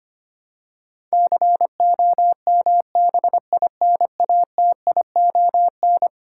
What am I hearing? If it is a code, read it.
Morse code, heard as a keyed tone, about 25 wpm, 700 Hz. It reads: COMBINATION
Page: morse-code-ninja-practice-25wpm